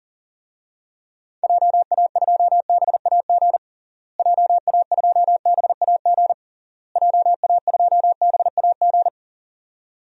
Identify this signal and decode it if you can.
Morse code, heard as a keyed tone, about 40 wpm, 700 Hz. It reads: JA2BAG JA2BAG JA2BAG
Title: DXCC-100 Call Signs - 6 Characters - Repeated 3x - 40wpm